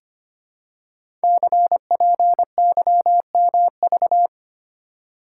Morse code, heard as a keyed tone, about 25 wpm, 700 Hz. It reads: CPYMV
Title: Code Groups - 5 Letter:Numbers - 25wpm